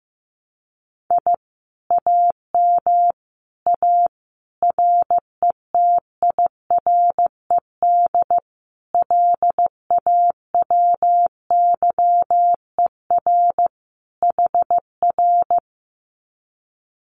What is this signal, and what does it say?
Morse code, heard as a keyed tone, about 15 wpm, 700 Hz. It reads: I AM A RETIRED LAWYER HR
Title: Common QSO Phrases Encore 15wpm